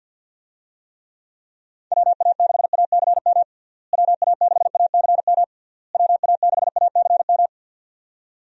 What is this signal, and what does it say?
Morse code, heard as a keyed tone, about 52 wpm, 700 Hz. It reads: WA6ACM WA6ACM WA6ACM
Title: USA Call Signs - 6 Characters - Repeated 3x - 50wpm